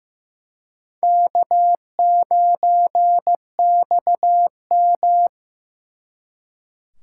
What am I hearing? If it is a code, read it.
Morse code, heard as a keyed tone, about 18 wpm, 700 Hz. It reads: K9XM
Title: Speed Racing - USA 1x2 Call Signs 15wpm